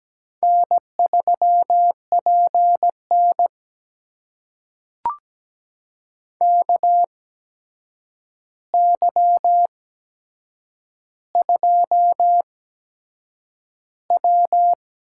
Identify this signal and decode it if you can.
Morse code, heard as a keyed tone, about 17 wpm, spaced as for 7 wpm, 700 Hz. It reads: N3PN KY2W
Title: USA Call Signs - 4 Characters - 8x Character Spacing - 17wpm